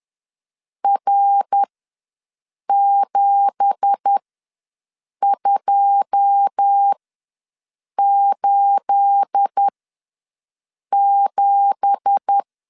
Morse code, heard as a keyed tone, about 10 wpm, spaced as for 6 wpm, 800 Hz. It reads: R7287
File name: LEZIONI / 9 2867 ws 10wpm 4215